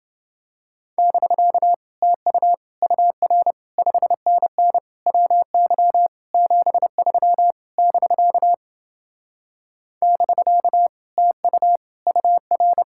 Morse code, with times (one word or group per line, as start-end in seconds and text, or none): 1.0-1.7 <BK>
2.0-2.5 TU
2.8-3.5 UR
3.8-4.8 5NN
5.1-6.1 WY
6.3-7.5 73
7.8-8.5 <BK>
10.0-10.9 <BK>
11.2-11.8 TU
12.1-12.8 UR